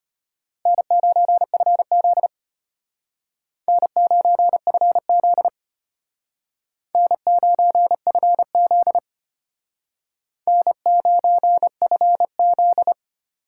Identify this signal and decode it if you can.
Morse code, heard as a keyed tone, about 32 wpm, 700 Hz. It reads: N9FZ N9FZ N9FZ N9FZ